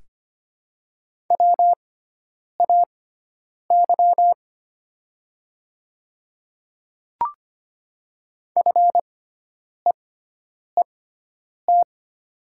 Morse code, heard as a keyed tone, about 25 wpm, spaced as for 9 wpm, 700 Hz. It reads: WAY FEET
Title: 6x Character Spacing - Top 300 Words - 25wpm